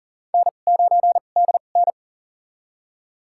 Morse code, heard as a keyed tone, about 40 wpm, spaced as for 29 wpm, 700 Hz. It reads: N9DN
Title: USA Call Signs - 2x Character Spacing - 40wpm